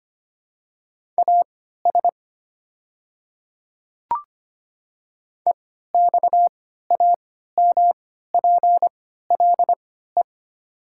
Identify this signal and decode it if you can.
Morse code, heard as a keyed tone, about 25 wpm, spaced as for 14 wpm, 690 Hz. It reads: AS EXAMPLE
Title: 3x Character Spacing - Top 300 Words - 25wpm